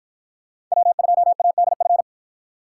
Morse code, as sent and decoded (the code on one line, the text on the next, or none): .-- ..--- .- -.. ..-.
W2ADF